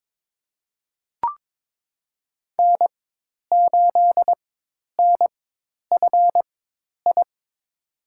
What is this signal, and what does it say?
Morse code, heard as a keyed tone, about 22 wpm, spaced as for 10 wpm, 700 Hz. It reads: N8NFI
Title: USA Call Signs - 5 Characters - 4x Character Spacing - 22wpm